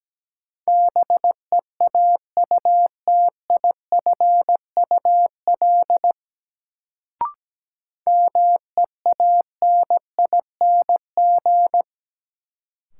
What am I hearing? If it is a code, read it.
Morse code, heard as a keyed tone, about 17 wpm, 700 Hz. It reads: BEAUTIFUL MEANING
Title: Sentences from Top 900 Words - New Word Drill - 17wpm